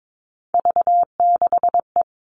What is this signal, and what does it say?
Morse code, heard as a keyed tone, about 22 wpm, 700 Hz. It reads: V6E
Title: All Cut Numbers - 3-Digit 22wpm